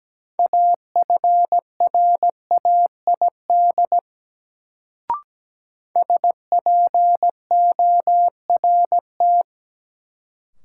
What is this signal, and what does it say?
Morse code, heard as a keyed tone, about 17 wpm, 700 Hz. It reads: AFRAID SPORT
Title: Sentences from Top 1500 Words - New Word Drill Alt - 17wpm